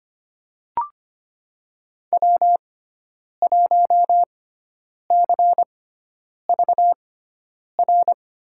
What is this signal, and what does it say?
Morse code, heard as a keyed tone, about 25 wpm, spaced as for 9 wpm, 700 Hz. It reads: W1CVR
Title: USA Call Signs - 5 Characters - 6x Character Spacing - 25wpm